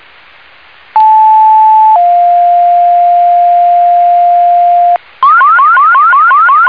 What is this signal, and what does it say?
Two-tone page, heard as a one-second tone, then a three-second tone, about 870 then 700 Hz, with noise. It is heard from 0.9 to 5.0 s.